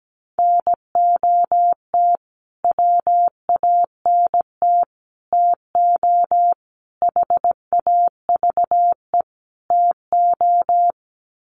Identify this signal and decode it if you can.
Morse code, heard as a keyed tone, about 17 wpm, 700 Hz. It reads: NOT WANT TO HAVE TO
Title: Sets of 5 Words Encore2 17wpm